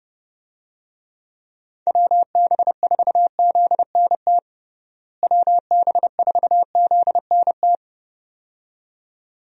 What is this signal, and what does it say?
Morse code, heard as a keyed tone, about 32 wpm, 700 Hz. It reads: WB4ZNT WB4ZNT
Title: USA Call Signs - 6 Characters - Repeated 2x - 30wpm